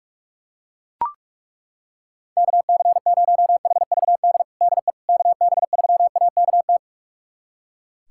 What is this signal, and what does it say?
Morse code, heard as a keyed tone, about 41 wpm, 700 Hz. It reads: KC0HVD DE KB3AKT
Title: USA Callsign DE Callsign 45wpm